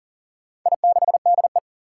Morse code, heard as a keyed tone, about 40 wpm, 700 Hz. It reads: I6DE
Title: Four Challenging Letter-Number Combinations 40wpm